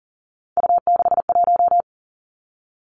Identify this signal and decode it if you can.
Morse code, heard as a keyed tone, about 40 wpm, 700 Hz. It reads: U61
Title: Three Challenging Letter-Number Combinations 40wpm